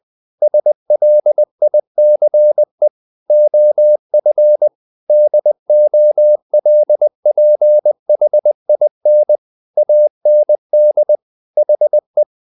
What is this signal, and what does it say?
Morse code, heard as a keyed tone, about 20 wpm, 600 Hz. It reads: SLICE OF DOLPHIN AND HE
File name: the old man and the sea part 2 20wpm